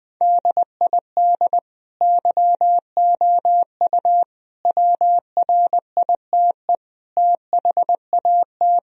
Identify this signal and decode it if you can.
Morse code, heard as a keyed tone, about 21 wpm, 700 Hz. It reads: DID YOU WRITE THAT
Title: Sentences from Top 1000 Words - Review Part 3 - 20wpm